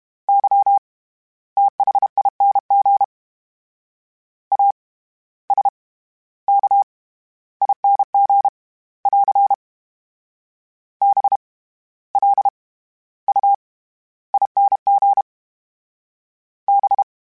Morse code, heard as a keyed tone, about 32 wpm, 800 Hz. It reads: Y THING A S K ING <AR> B L U ING B
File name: Special INGs / Special INGs-mp3s / SlowToFaster ing @30